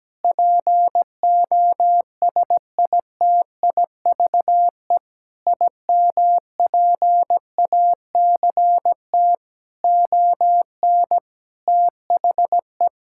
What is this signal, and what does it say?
Morse code, heard as a keyed tone, about 17 wpm, 700 Hz. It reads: POSITIVE IMPACT ON THE
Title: Sentences from Top 1000 Words - Review Part 6 - 17wpm